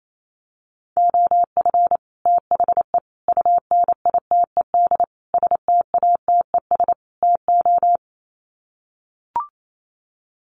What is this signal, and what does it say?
Morse code, heard as a keyed tone, about 28 wpm, 700 Hz. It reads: OF THE UNITED STATES TO
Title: Sets of 5 Words Encore 28wpm